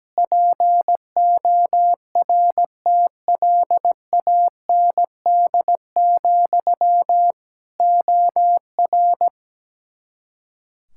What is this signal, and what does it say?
Morse code, heard as a keyed tone, about 17 wpm, 700 Hz. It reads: PORTLAND, OR
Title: K1USN SST QSO Practice 17wpm